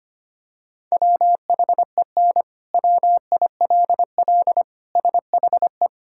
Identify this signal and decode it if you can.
Morse code, heard as a keyed tone, about 25 wpm, 700 Hz. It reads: WHEN WILL SHE
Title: Sentences from Top 1000 Words - Review Part 5 - 25wpm